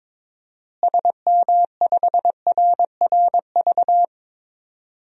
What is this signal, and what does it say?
Morse code, heard as a keyed tone, about 22 wpm, 700 Hz. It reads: SM5RRV
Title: DXCC-100 Call Signs - 6 Characters - Repeated 3x - 22wpm